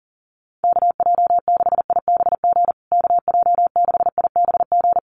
Morse code, heard as a keyed tone, about 37 wpm, 700 Hz. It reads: KJ6IBG KJ6IBG
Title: USA Call Signs Repeated 2x 40wpm